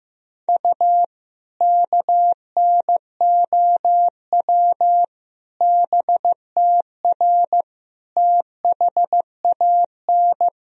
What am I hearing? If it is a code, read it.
Morse code, heard as a keyed tone, about 15 wpm, 700 Hz. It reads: U KNOW BTR THAN
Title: Sentences from Top 400 Words and Abbreviations 15wpm